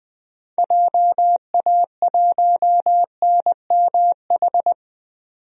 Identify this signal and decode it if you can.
Morse code, heard as a keyed tone, about 20 wpm, 700 Hz. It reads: JA1NMH
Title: DXCC-100 Call Signs - 6 Characters - 20wpm